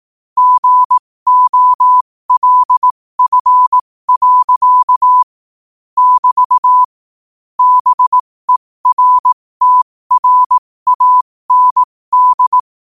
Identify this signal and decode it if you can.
Morse code, heard as a keyed tone, about 18 wpm, spaced as for 15 wpm, 1000 Hz. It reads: GOLF. <BT> BERTRAND
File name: 15wpm